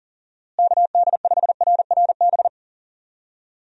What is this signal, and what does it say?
Morse code, heard as a keyed tone, about 40 wpm, 700 Hz. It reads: KD5RRB
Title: USA Call Signs - 6 Characters - Repeated 2x - 40wpm